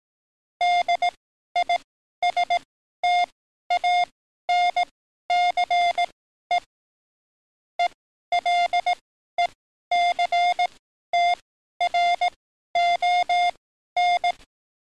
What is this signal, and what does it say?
Morse code, heard as a keyed tone, about 18 wpm, spaced as for 12 wpm, 710 Hz. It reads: DISTANCE ELECTRON
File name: training / NAVPERS 11049to11060 / NAVPERS 11053 RA RB